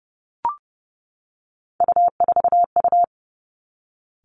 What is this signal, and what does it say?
Morse code, heard as a keyed tone, about 32 wpm, 700 Hz. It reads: U4U
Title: Three Challenging Letter-Number Combinations 30wpm